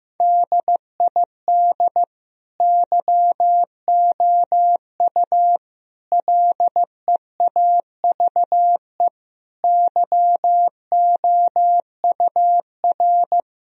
Morse code, heard as a keyed tone, about 16 wpm, 700 Hz. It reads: DID YOU LEAVE YOUR
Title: Sentences from Top 1000 Words - Review Part 3 - 15wpm